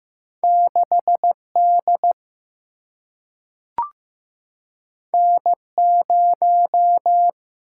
Morse code, heard as a keyed tone, about 15 wpm, 700 Hz. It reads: 6D N0
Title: Single-Letter Single-Number 15wpm